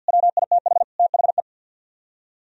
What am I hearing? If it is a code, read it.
Morse code, heard as a keyed tone, about 64 wpm, 690 Hz. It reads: WITH THE